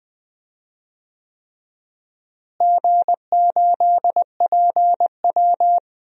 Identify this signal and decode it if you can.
Morse code, heard as a keyed tone, about 20 wpm, 700 Hz. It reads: G8PW
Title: DXCC-100 Call Signs - 4 Characters - Repeated 3x - 20wpm